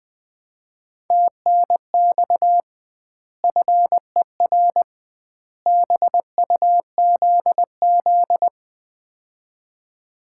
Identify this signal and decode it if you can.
Morse code, heard as a keyed tone, about 21 wpm, 700 Hz. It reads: TNX FER BUZZ
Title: Common QSO Phrases - 2x Word-Spacing - 20wpm